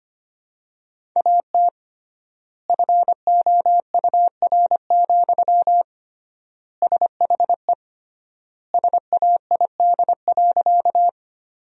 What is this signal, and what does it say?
Morse code, heard as a keyed tone, about 25 wpm, 700 Hz. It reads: AT FOUR, SHE SAID.
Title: 3x Word-Spacing - Sentences from Top 300 Words - 25wpm